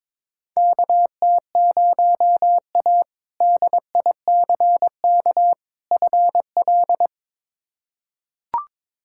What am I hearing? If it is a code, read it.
Morse code, heard as a keyed tone, about 21 wpm, 700 Hz. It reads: KT0A DICK FL
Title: K1USN SST QSO Practice - 22wpm